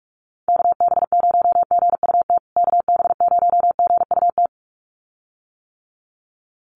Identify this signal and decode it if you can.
Morse code, heard as a keyed tone, about 44 wpm, 700 Hz. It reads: KB0GUT KB0GUT
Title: USA Call Signs Repeated 2x 45wpm